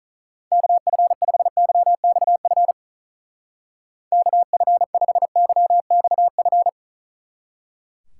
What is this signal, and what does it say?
Morse code, heard as a keyed tone, about 45 wpm, 700 Hz. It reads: KF5YXF KF5YXF